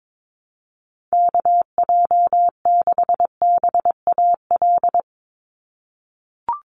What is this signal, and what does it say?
Morse code, heard as a keyed tone, about 22 wpm, 700 Hz. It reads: KJ6BAL
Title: USA Call Signs 22wpm